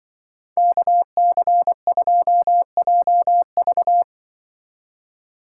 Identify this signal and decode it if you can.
Morse code, heard as a keyed tone, about 23 wpm, 700 Hz. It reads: KC2JV
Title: Speed Racing - USA 2x2 Call Signs 20wpm